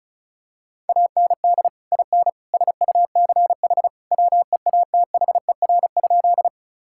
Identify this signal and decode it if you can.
Morse code, heard as a keyed tone, about 36 wpm, 700 Hz. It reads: AND IN SUCH WEATHER?
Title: Sentences from Top 1000 Words - Review Part 5 - 35wpm